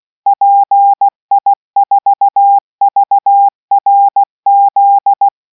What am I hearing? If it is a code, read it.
Morse code, heard as a keyed tone, about 16 wpm, 800 Hz. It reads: PI4VRZ